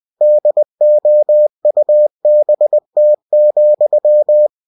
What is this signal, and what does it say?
Morse code, heard as a keyed tone, about 20 wpm, 600 Hz. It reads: DOUBT,
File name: cw-books / winnie-the-pooh / winnie-20wpm-page-0105